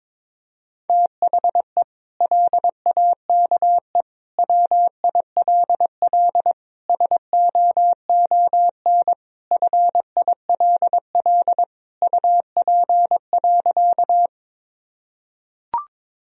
Morse code, heard as a keyed tone, about 22 wpm, 700 Hz. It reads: THE LAKE WILL SOON FILL UP.
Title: Sentences from Top 1000 Words - Review Part 3 - 22wpm